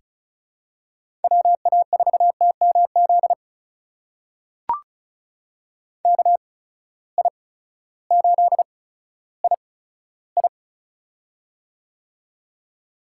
Morse code, heard as a keyed tone, about 36 wpm, spaced as for 16 wpm, 700 Hz. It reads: WA4TMZ KI8II
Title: USA Call Signs - 8x Character Spacing - 35wpm